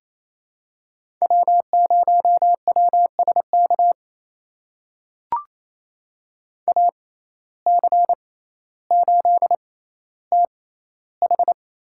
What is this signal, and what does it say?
Morse code, heard as a keyed tone, about 28 wpm, spaced as for 14 wpm, 700 Hz. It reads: W0WSK AC8TH
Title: USA Call Signs - 6x Character Spacing - 28wpm